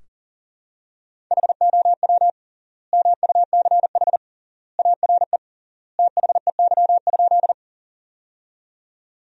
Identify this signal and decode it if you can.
Morse code, heard as a keyed tone, about 40 wpm, 700 Hz. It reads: HOW MUCH ARE THEY?
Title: 3x Word-Spacing - Sentences from Top 200 Words - 40wpm